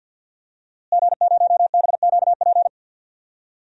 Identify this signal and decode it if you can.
Morse code, heard as a keyed tone, about 49 wpm, 700 Hz. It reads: G0BQP
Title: DXCC-100 Call Signs - 5 Characters - Repeated 3x - 50wpm